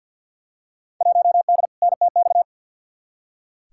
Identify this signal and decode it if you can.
Morse code, heard as a keyed tone, about 47 wpm, 700 Hz. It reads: 1D NTX
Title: Field Day Exchanges - 50wpm